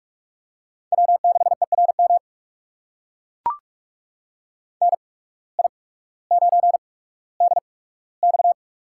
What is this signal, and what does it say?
Morse code, heard as a keyed tone, about 44 wpm, spaced as for 19 wpm, 700 Hz. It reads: W6ERM NI9DX
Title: USA Call Signs - 8x Character Spacing - 45wpm